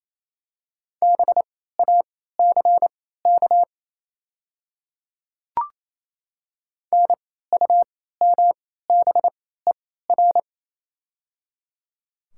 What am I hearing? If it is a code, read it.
Morse code, heard as a keyed tone, about 28 wpm, spaced as for 16 wpm, 700 Hz. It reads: BACK NUMBER
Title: 3x Character Spacing - Top 200 Words - 28wpm